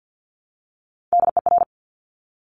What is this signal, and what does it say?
Morse code, heard as a keyed tone, about 50 wpm, 700 Hz. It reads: DER
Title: Top Three Letter Combinations in Words 50wpm